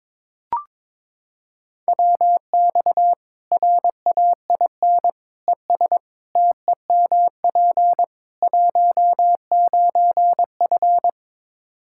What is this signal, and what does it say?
Morse code, heard as a keyed tone, about 22 wpm, 700 Hz. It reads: WX RAIN ES TEMP 19F
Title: Practice QSO - Exchange 2 - Info - 22wpm